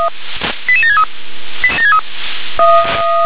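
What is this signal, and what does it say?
Five-tone selective call (ZVEI), heard as quick runs of five tones, tones of about 70 ms. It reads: 88763
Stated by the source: Five-tone ZVEI selective call received by SDR.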